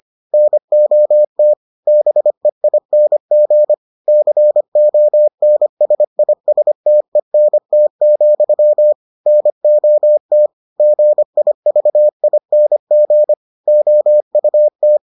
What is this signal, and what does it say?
Morse code, heard as a keyed tone, about 25 wpm, 600 Hz. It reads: NOT BEING CONSISTENT, NOT GIVING OUT